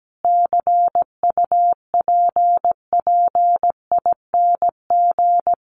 Morse code, heard as a keyed tone, about 17 wpm, 700 Hz. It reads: CUPPING
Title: A Sound Of Thunder 17wpm